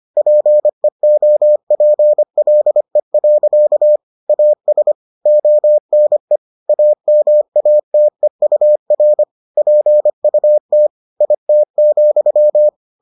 Morse code, heard as a keyed tone, about 25 wpm, 600 Hz. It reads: PEOPLE. AS ONE AMATEUR PUT IT,